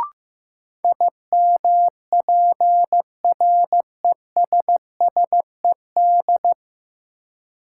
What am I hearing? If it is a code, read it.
Morse code, heard as a keyed tone, about 15 wpm, 700 Hz. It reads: IMPRESSED
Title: Sentences from Top 1600 Words - New Word Drill - 15wpm